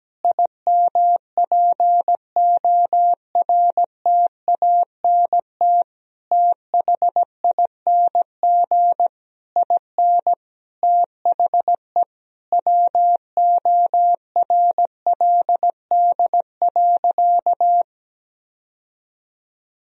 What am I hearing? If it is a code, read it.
Morse code, heard as a keyed tone, about 17 wpm, 700 Hz. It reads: IMPORTANT THING IN THE WORLD.
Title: Sentences from Top 1000 Words - Review Part 1 - 17wpm